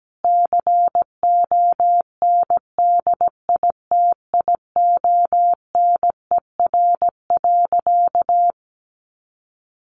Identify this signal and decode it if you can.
Morse code, heard as a keyed tone, about 17 wpm, 700 Hz. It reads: CONDITIONER.
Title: Easy Phrases 17wpm